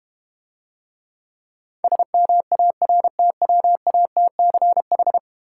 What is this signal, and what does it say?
Morse code, heard as a keyed tone, about 32 wpm, 700 Hz. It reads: SMARTWATCH